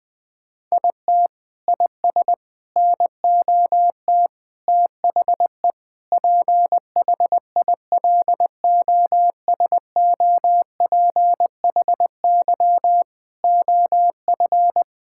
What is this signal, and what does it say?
Morse code, heard as a keyed tone, about 21 wpm, 700 Hz. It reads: IT IS NOT THE PHILOSOPHY OF
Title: Sentences from Top 1000 Words - Review Part 6 - 20wpm